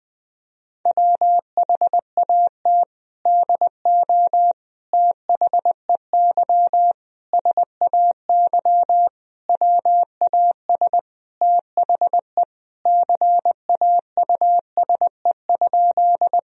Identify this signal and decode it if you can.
Morse code, heard as a keyed tone, about 20 wpm, 700 Hz. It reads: WHAT DO THEY SAY WAS THE CAUSE?
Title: Sentences from Top 1000 Words - Review Part 1 - 20wpm